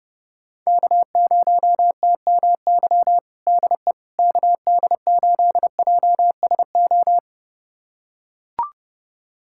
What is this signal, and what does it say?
Morse code, heard as a keyed tone, about 30 wpm, 700 Hz. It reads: K0TMY DE KD8JSO
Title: USA Callsign DE Callsign 30wpm